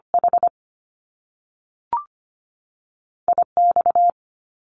Morse code, heard as a keyed tone, about 25 wpm, spaced as for 3 wpm, 700 Hz. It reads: HI X
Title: Top Two Letter Combinations in Words 25wpm